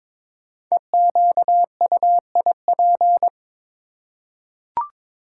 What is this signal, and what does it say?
Morse code, heard as a keyed tone, about 22 wpm, 690 Hz. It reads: EQUIP
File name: 5 Letter Words - First 3 Letters Given - 22wpm